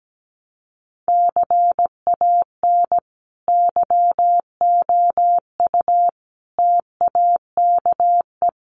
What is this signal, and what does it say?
Morse code, heard as a keyed tone, about 17 wpm, 700 Hz. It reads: CAN YOU TAKE
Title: Sentences from Top 200 Words 17wpm